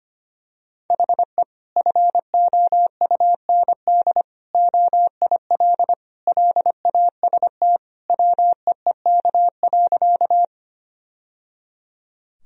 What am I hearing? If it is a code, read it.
Morse code, heard as a keyed tone, about 25 wpm, 700 Hz. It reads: HE FOUND OIL LAST WEEK.
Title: Sentences from Top 1000 Words - Review Part 4 - 25wpm